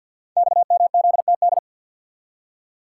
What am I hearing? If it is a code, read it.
Morse code, heard as a keyed tone, about 47 wpm, 700 Hz. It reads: XMZTD
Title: Code Groups - 5 Letter:Numbers - 50wpm